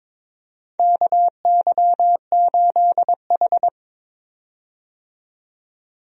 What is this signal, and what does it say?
Morse code, heard as a keyed tone, about 22 wpm, 700 Hz. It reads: KY8H
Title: USA Call Signs - 4 Characters - 22wpm